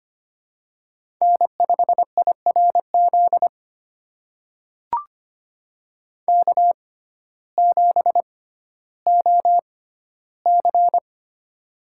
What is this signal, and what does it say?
Morse code, heard as a keyed tone, about 25 wpm, spaced as for 14 wpm, 700 Hz. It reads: N5IRZ K7OC
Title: USA Call Signs - 6x Character Spacing - 25wpm